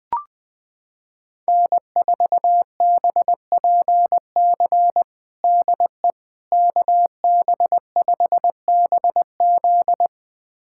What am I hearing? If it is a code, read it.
Morse code, heard as a keyed tone, about 20 wpm, 700 Hz. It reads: N4BPC DE KB5BZ
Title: USA Callsign DE Callsign - 5 Characters - 20wpm